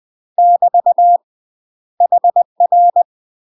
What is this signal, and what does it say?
Morse code, heard as a keyed tone, about 20 wpm, 700 Hz. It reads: <BT> HR
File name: UK MASTER QSOs TRAINING / 20WPM / UK MASTER QSOs 0061